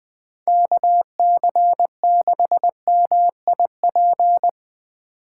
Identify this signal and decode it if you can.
Morse code, heard as a keyed tone, about 22 wpm, 700 Hz. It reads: KC6MIP
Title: Speed Racing - USA 2x3 Call Signs 20wpm